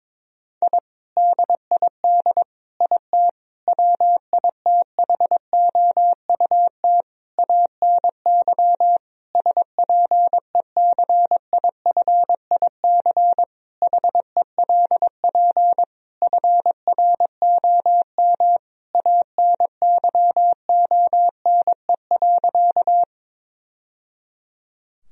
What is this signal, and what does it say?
Morse code, heard as a keyed tone, about 22 wpm, 700 Hz. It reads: I DID IT WITHOUT ANY SPECIFIC HELP FROM ANYONE.
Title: Sentences from Top 1000 Words - Review Part 3 - 22wpm